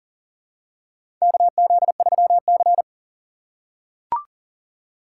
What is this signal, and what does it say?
Morse code, heard as a keyed tone, about 40 wpm, 700 Hz. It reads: KZ3C
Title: USA Call Signs - 4 Characters - 4x Character Spacing - 40wpm